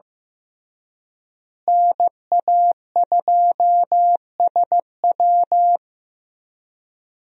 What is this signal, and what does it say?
Morse code, heard as a keyed tone, about 15 wpm, 700 Hz. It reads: NA2SW
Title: USA Call Signs - 5 Characters - Repeated 3x - 15wpm